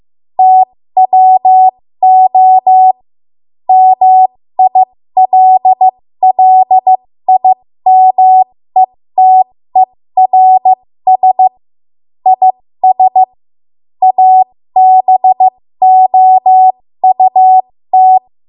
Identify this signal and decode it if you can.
Morse code, heard as a keyed tone, about 15 wpm, spaced as for 13 wpm, 750 Hz. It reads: TWO MILLIMETERS IS ABOUT
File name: Morse / 13WPM